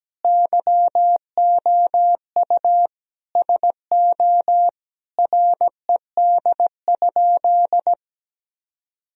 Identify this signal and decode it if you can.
Morse code, heard as a keyed tone, about 17 wpm, 700 Hz. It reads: YOU SO RED?
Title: Sentences from Top 1000 Words - Review Part 2 - 17wpm